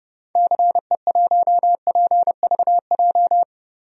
Morse code, heard as a keyed tone, about 32 wpm, 700 Hz. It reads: CE1PVJ
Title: DXCC-100 Call Signs - 6 Characters - Repeated 3x - 30wpm